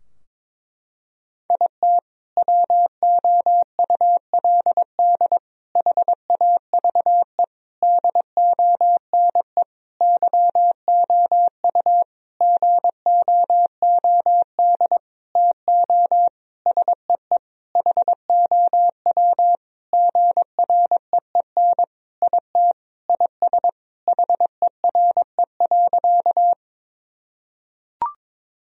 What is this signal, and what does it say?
Morse code, heard as a keyed tone, about 21 wpm, 700 Hz. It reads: IT WOULD HAVE DONE YOU GOOD TO SEE HOW GREEN IT IS HERE.
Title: Sentences from Top 1000 Words - Review Part 2 - 22wpm